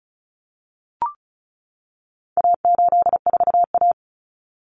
Morse code, heard as a keyed tone, about 34 wpm, 700 Hz. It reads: A84A
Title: Common Cut Numbers - 4-Digit 35wpm